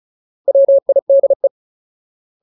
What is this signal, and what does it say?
Morse code, heard as a keyed tone, about 34 wpm, 550 Hz. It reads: WIDE
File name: Ultimate Word List / 35WPM Ultimate